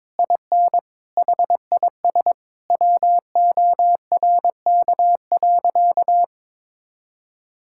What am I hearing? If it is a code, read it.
Morse code, heard as a keyed tone, about 22 wpm, 700 Hz. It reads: IN HIS WORK.
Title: Sentences from Top 1000 Words - Review Part 6 - 22wpm